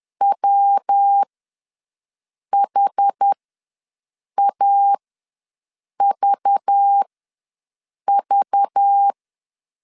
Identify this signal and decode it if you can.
Morse code, heard as a keyed tone, about 10 wpm, spaced as for 6 wpm, 800 Hz. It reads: WHAVV
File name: LEZIONI / 5 LBVC / 5 LBV ws 10wpm 3645